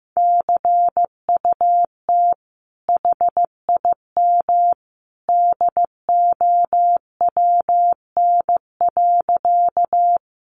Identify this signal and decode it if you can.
Morse code, heard as a keyed tone, about 15 wpm, 700 Hz. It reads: CUT HIM DOWN.
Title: Sentences from Top 300 Words 15wpm